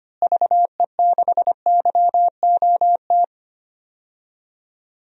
Morse code, heard as a keyed tone, about 25 wpm, 700 Hz. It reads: VE6YOT
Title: DXCC-100 Call Signs - 6 Characters - Repeated 3x - 25wpm